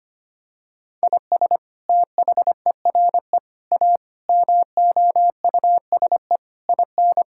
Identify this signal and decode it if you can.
Morse code, heard as a keyed tone, about 25 wpm, 700 Hz. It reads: IS THERE A MOUSE IN
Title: Sentences from Top 1000 Words - Review Part 5 - 25wpm